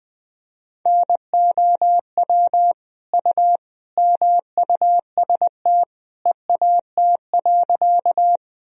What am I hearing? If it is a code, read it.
Morse code, heard as a keyed tone, about 20 wpm, 700 Hz. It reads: NOW U MUST EAT.
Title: Sentences from Top 300 Words and Abbreviations 20wpm